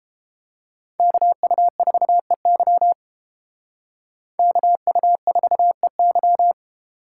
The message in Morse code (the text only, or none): KU4EY KU4EY